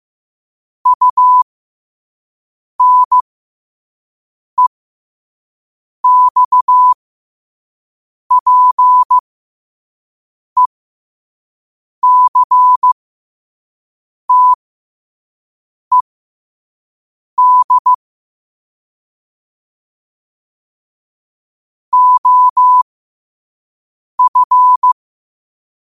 Morse code, heard as a keyed tone, about 15 wpm, spaced as for 5 wpm, 1000 Hz. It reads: UNEXPECTED OF
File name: qotdfiles / 5wpm